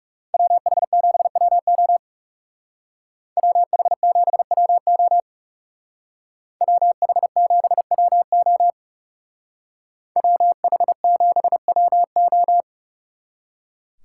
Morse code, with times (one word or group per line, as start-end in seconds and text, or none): 0.3-2.0 WH7WO
3.4-5.2 WH7WO
6.6-8.7 WH7WO
10.2-12.6 WH7WO